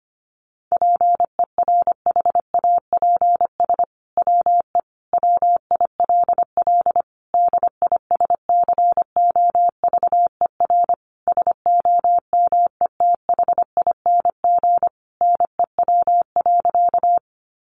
Morse code, heard as a keyed tone, about 25 wpm, 700 Hz. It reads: PERHAPS WE WILL DISCOVER SOMETHING NEW.
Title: Sentences from Top 1100 Words 25wpm